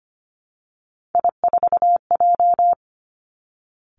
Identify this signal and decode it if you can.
Morse code, heard as a keyed tone, about 25 wpm, 700 Hz. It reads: I4J
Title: Three Challenging Letter-Number Combinations 25wpm